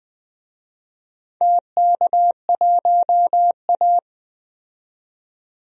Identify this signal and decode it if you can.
Morse code, heard as a keyed tone, about 20 wpm, 700 Hz. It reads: TK1A
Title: DXCC-100 Call Signs - 4 Characters - 20wpm